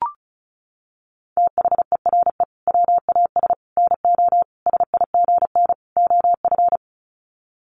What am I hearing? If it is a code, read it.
Morse code, heard as a keyed tone, about 34 wpm, 700 Hz. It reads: THERE WAS NO SIGN OF
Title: Sets of 5 Words Encore2 35wpm